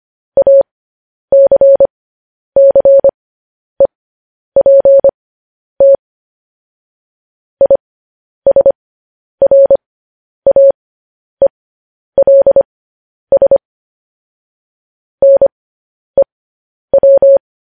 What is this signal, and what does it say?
Morse code, heard as a keyed tone, about 25 wpm, spaced as for 10 wpm, 550 Hz. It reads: ACCEPT ISRAELS NEW